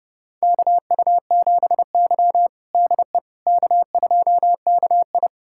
Morse code, heard as a keyed tone, about 30 wpm, 700 Hz. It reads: KU7Y DE K2KI
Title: USA Callsign DE Callsign - 4 Characters - 30wpm